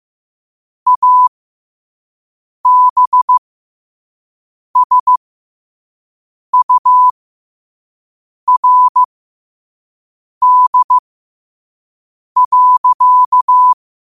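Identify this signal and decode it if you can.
Morse code, heard as a keyed tone, about 15 wpm, spaced as for 5 wpm, 1000 Hz. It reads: ABSURD.
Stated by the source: Morse code quote of the day at 5 wpm.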